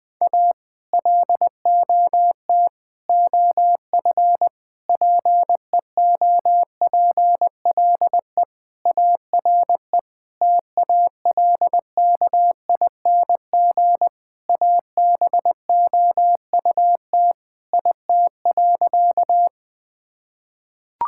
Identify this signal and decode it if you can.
Morse code, heard as a keyed tone, about 20 wpm, 700 Hz. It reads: A LOT OF PEOPLE ARE TALKING ABOUT IT.
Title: Sentences with 5-Word Phrases - 20wpm